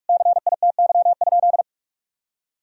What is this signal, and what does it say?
Morse code, heard as a keyed tone, about 46 wpm, 700 Hz. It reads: CITY?
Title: Sentences from Top 1000 Words - Review Part 1 - 45wpm